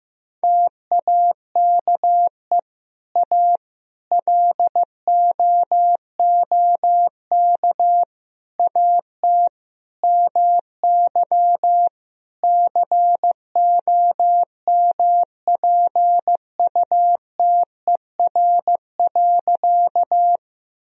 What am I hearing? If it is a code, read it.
Morse code, heard as a keyed tone, about 15 wpm, 700 Hz. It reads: TAKE A LOOK AT MY COMPUTER.
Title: Sentences from Top 1000 Words - Review Part 3 - 15wpm